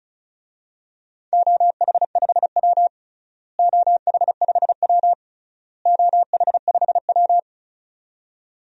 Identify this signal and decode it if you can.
Morse code, heard as a keyed tone, about 36 wpm, 700 Hz. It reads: OH5W OH5W OH5W
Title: DXCC-100 Call Signs - 4 Characters - Repeated 3x - 35wpm